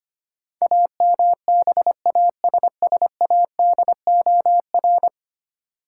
Morse code, heard as a keyed tone, about 25 wpm, 700 Hz. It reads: AMBASSADOR
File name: Ten Letter Words - 25wpm